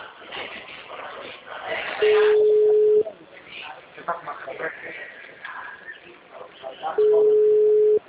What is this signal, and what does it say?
Ringback tone (Europe, 425 Hz), with speech.